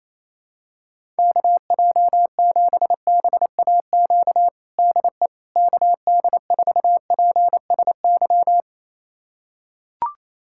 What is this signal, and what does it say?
Morse code, heard as a keyed tone, about 28 wpm, 700 Hz. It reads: KJ7BAQ DE KD4PSY
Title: USA Callsign DE Callsign 28wpm